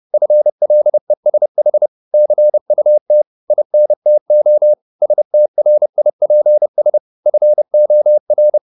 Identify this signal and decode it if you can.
Morse code, heard as a keyed tone, about 30 wpm, 600 Hz. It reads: FLESH CUT INTO STRIPS FOR
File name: the old man and the sea / the old man and the sea 1 30wpm